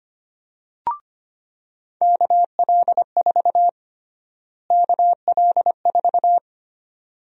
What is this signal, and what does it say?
Morse code, heard as a keyed tone, about 25 wpm, 700 Hz. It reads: KL4 KL4
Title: USA Call Sign Prefixes - Repeated 2x - 25wpm